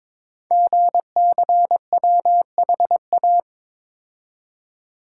Morse code, heard as a keyed tone, about 22 wpm, 700 Hz. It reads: GCWHA
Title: Code Groups - 5 Letter:Numbers - 22wpm